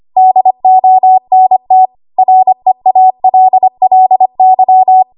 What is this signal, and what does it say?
Morse code, heard as a keyed tone, about 25 wpm, 750 Hz. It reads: DONT REALLY
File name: Archive / 25 WPM / 25WPM